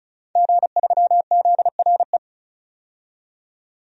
Morse code, heard as a keyed tone, about 34 wpm, 700 Hz. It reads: G3ZRE
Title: DXCC-100 Call Signs - 5 Characters - 35wpm